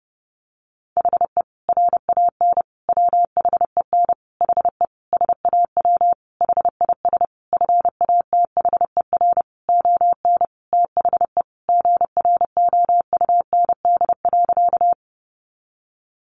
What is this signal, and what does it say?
Morse code, heard as a keyed tone, about 30 wpm, 700 Hz. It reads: HE RAN WHEN HE SAW HIS FATHER ON THE GROUND.
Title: Sentences from Top 500 Words 30wpm